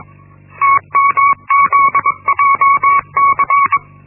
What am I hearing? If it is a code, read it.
Morse code, heard as a keyed tone, about 21 wpm, 1120 Hz. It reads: TMZJC
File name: Linksmakalnis-1